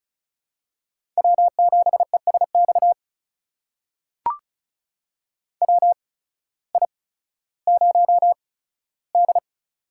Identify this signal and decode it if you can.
Morse code, heard as a keyed tone, about 36 wpm, spaced as for 16 wpm, 700 Hz. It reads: W7ESX WI0D
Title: USA Call Signs - 8x Character Spacing - 35wpm